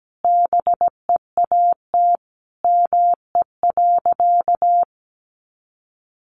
Morse code, heard as a keyed tone, about 17 wpm, 700 Hz. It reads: BEAT ME.
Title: Sentences from Top 1300 Words 17wpm